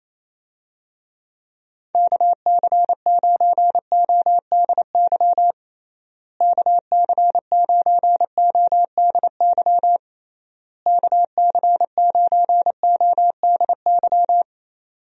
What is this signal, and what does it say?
Morse code, heard as a keyed tone, about 28 wpm, 700 Hz. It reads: KC9ODY KC9ODY KC9ODY
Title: USA Call Signs - 6 Characters - Repeated 3x - 28wpm